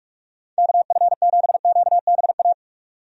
Morse code, heard as a keyed tone, about 43 wpm, 700 Hz. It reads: KF7QBA
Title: USA Call Signs - 6 Characters - 45wpm